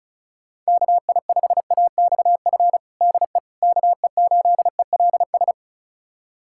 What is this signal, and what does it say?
Morse code, heard as a keyed tone, about 36 wpm, 700 Hz. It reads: KI5AXF DE KE8ELS
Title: USA Callsign DE Callsign - 6 Characters - 35wpm